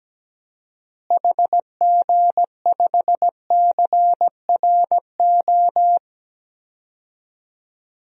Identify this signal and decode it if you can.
Morse code, heard as a keyed tone, about 17 wpm, 700 Hz. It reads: HG5CRO
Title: DXCC-100 Call Signs - 6 Characters - Repeated 3x - 17wpm